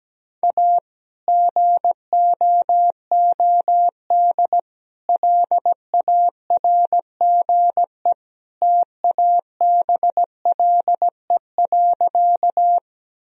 Morse code, heard as a keyed tone, about 17 wpm, 700 Hz. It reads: A GOOD LARGE TABLE.
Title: Sentences from Top 1000 Words - Review Part 2 - 17wpm